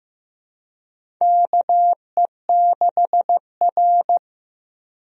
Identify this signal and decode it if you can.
Morse code, heard as a keyed tone, about 16 wpm, 700 Hz. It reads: KE6R
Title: USA Call Signs - 4 Characters - Repeated 3x - 15wpm